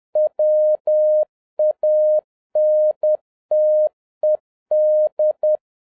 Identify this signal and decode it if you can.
Morse code, heard as a keyed tone, about 10 wpm, 610 Hz. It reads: WANTED